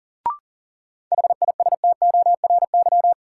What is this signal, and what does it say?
Morse code, heard as a keyed tone, about 37 wpm, 700 Hz. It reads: HISTORY
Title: Sets of 3 Words - New Word Drill - 40wpm